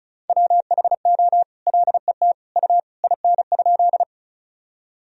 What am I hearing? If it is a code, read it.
Morse code, heard as a keyed tone, about 36 wpm, 700 Hz. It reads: WHO LET U IN?
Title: Sentences from Top 300 Words and Abbreviations 35wpm